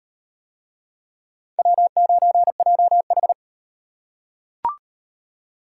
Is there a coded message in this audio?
Yes: Morse code from 1.6 to 3.3 s, heard as a keyed tone, about 33 wpm, 700 Hz. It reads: W9JH